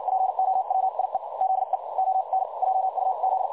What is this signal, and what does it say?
Morse code, heard as a keyed tone, about 56 wpm, spaced as for 47 wpm, 760 Hz. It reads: POWIE ZE ON JUZ
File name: dlazaawansowanych / nagraniazpasma / websdr recording 3565.0kHz